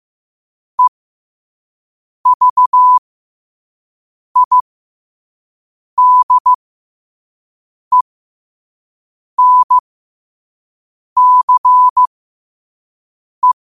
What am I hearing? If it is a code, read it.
Morse code, heard as a keyed tone, about 15 wpm, spaced as for 5 wpm, 1000 Hz. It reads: EVIDENCE
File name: qotdfiles / 5wpm